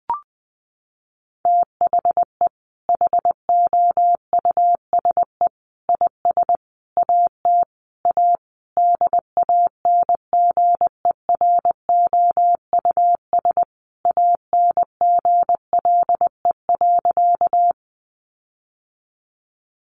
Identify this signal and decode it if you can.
Morse code, heard as a keyed tone, about 20 wpm, 700 Hz. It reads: THE HOUSE IS AT A DANGEROUS ANGLE.
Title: Sentences from Top 1500 Words 20wpm